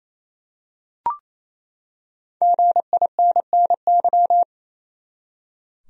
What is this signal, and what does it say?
Morse code, heard as a keyed tone, about 28 wpm, 700 Hz. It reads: GINNY
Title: USA Common Names - 28wpm